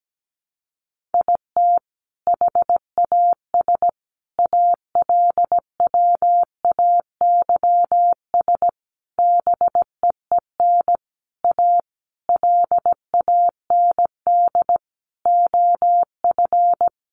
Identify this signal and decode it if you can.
Morse code, heard as a keyed tone, about 17 wpm, 700 Hz. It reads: IT HAS ALWAYS BEEN A LAND OF
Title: Sentences from Top 1300 Words 17wpm